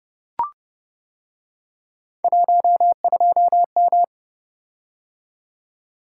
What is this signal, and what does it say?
Morse code, heard as a keyed tone, about 29 wpm, 700 Hz. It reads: 12M